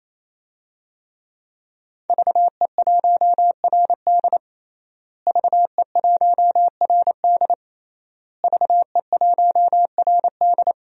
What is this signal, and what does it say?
Morse code, heard as a keyed tone, about 27 wpm, 700 Hz. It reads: VE1RD VE1RD VE1RD
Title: DXCC-100 Call Signs - 5 Characters - Repeated 3x - 28wpm